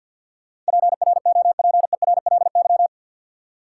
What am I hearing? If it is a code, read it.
Morse code, heard as a keyed tone, about 53 wpm, 700 Hz. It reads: PROPERLY
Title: Sentences from Top 1100 Words - New Word Drill - 50wpm